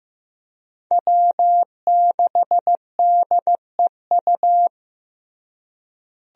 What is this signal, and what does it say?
Morse code, heard as a keyed tone, about 15 wpm, 700 Hz. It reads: W6DEU
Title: USA Call Signs - 5 Characters - Repeated 3x - 15wpm